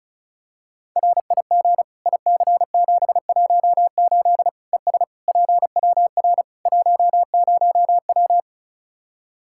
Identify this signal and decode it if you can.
Morse code, heard as a keyed tone, about 34 wpm, 700 Hz. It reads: RIG IC718 ES PWR 10W
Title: Practice QSO - Exchange 2 - Info - 35wpm